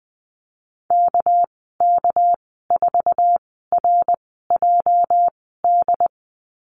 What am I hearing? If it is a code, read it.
Morse code, heard as a keyed tone, about 20 wpm, spaced as for 15 wpm, 700 Hz. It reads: KK4RJD
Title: 238 Lesson 38g - Letter q - Call Signs 20wpm